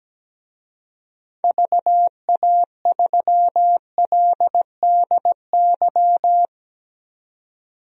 Morse code, heard as a keyed tone, about 17 wpm, 700 Hz. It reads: VA3LDY
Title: DXCC-100 Call Signs - 6 Characters - 17wpm